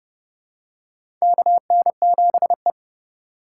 Morse code, heard as a keyed tone, about 32 wpm, 700 Hz. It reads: KN7E
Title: USA Call Signs - 4 Characters - 3x Character Spacing - 30wpm